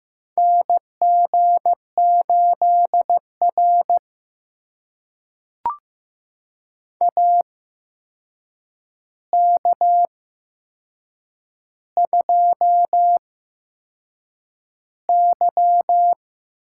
Morse code, heard as a keyed tone, about 15 wpm, spaced as for 6 wpm, 700 Hz. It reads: NG8R AK2Y
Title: USA Call Signs - 4 Characters - 8x Character Spacing - 15wpm